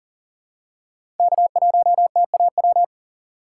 Morse code, heard as a keyed tone, about 38 wpm, 700 Hz. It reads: K1TAW
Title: USA Call Signs - 5 Characters - 2x Character Spacing - 40wpm